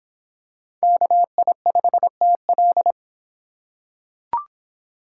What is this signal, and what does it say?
Morse code, heard as a keyed tone, about 26 wpm, 700 Hz. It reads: KI5TL